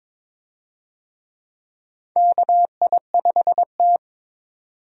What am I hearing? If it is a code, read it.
Morse code, heard as a keyed tone, about 22 wpm, 700 Hz. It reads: KI5T
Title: USA Call Signs - 4 Characters - 22wpm